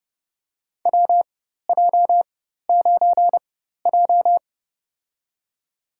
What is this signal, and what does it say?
Morse code, heard as a keyed tone, about 30 wpm, spaced as for 14 wpm, 700 Hz. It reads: WJ9J
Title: USA Call Signs - 4 Characters - 4x Character Spacing - 30wpm